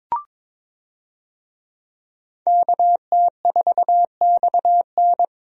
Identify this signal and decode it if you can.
Morse code, heard as a keyed tone, about 22 wpm, 700 Hz. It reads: KT4XN
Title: USA Call Signs - 5 Characters - Repeated 3x - 22wpm